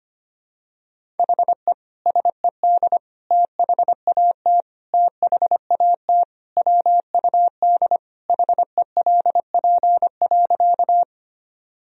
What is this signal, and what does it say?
Morse code, heard as a keyed tone, about 25 wpm, 700 Hz. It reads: HE SED THAT THAT WUD HELP.
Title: Sentences from Top 200 Words and Abbreviations 25wpm